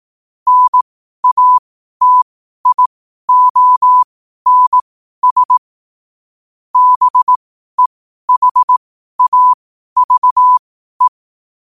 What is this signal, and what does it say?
Morse code, heard as a keyed tone, about 18 wpm, spaced as for 12 wpm, 1000 Hz. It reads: NATIONS BEHAVE
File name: qotdfiles / 12wpm